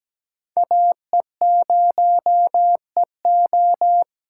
Morse code, heard as a keyed tone, about 17 wpm, 700 Hz. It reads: AE0EO
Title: USA Call Signs - 5 Characters - 8x Character Spacing - 17wpm